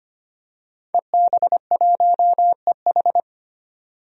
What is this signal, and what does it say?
Morse code, heard as a keyed tone, about 25 wpm, 700 Hz. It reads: EB1EH
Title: DXCC-100 Call Signs - 5 Characters - 25wpm